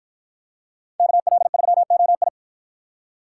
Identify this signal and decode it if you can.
Morse code, heard as a keyed tone, about 53 wpm, 700 Hz. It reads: KL3OI